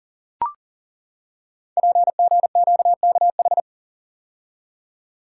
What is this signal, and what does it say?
Morse code, heard as a keyed tone, about 42 wpm, 700 Hz. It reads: PGQKH
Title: Code Groups - 5 Letters - 40wpm